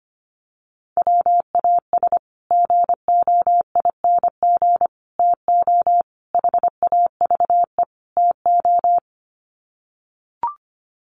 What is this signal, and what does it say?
Morse code, heard as a keyed tone, about 25 wpm, 700 Hz. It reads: WAS GOING TO HAVE TO
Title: Sets of 5 Words Encore 25wpm